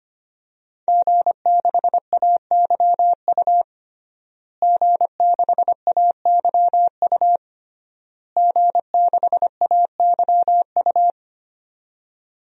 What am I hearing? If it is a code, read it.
Morse code, heard as a keyed tone, about 25 wpm, 700 Hz. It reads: G6AYU G6AYU G6AYU
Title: DXCC-100 Call Signs - 5 Characters - Repeated 3x - 25wpm